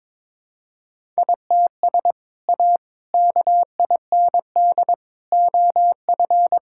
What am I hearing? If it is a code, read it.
Morse code, heard as a keyed tone, about 22 wpm, 700 Hz. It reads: ITS A KIND OF
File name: practice / 22wpm / Sentences with 3-Word Phrases - 22wpm